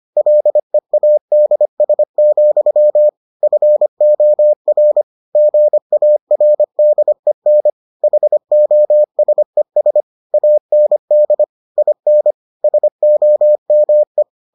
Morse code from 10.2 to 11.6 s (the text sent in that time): AND